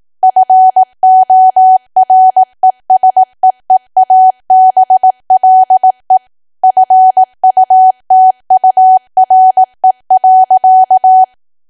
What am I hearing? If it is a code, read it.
Morse code, heard as a keyed tone, about 18 wpm, 750 Hz. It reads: FORESEEABLE FUTURE.
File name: cwa / W1AW-Level2 / 18WPM